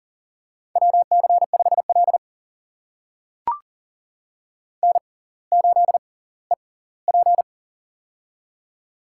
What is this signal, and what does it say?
Morse code, heard as a keyed tone, about 40 wpm, spaced as for 21 wpm, 700 Hz. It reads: WC5L N8EP
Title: USA Call Signs - 4 Characters - 6x Character Spacing - 40wpm